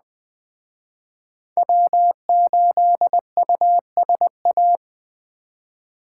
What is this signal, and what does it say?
Morse code, heard as a keyed tone, about 21 wpm, 700 Hz. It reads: W8USA